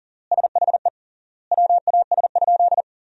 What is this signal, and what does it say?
Morse code, heard as a keyed tone, about 40 wpm, 700 Hz. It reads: SHE WAS?
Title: 3x Word-Spacing - Sentences from Top 200 Words - 40wpm